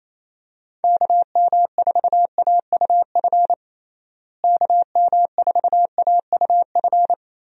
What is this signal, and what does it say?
Morse code, heard as a keyed tone, about 27 wpm, 700 Hz. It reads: KM4AUF KM4AUF
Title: USA Call Signs - 6 Characters - Repeated 3x - 28wpm